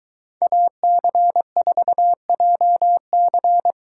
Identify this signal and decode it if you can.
Morse code, heard as a keyed tone, about 22 wpm, 700 Hz. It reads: AC4JC